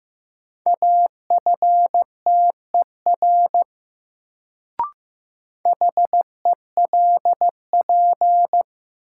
Morse code, heard as a keyed tone, about 16 wpm, 700 Hz. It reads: AFTER HELP
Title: Sets of 2 Words - New Word Drill - 15wpm